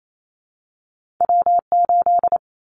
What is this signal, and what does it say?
Morse code, heard as a keyed tone, about 28 wpm, 700 Hz. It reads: W8
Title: Callsign Prefixes for Top 25 DXCC - 28wpm